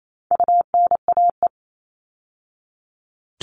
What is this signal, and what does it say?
Morse code, heard as a keyed tone, about 28 wpm, 700 Hz. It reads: UNAE
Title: All Cut Numbers - 4-Digit 28wpm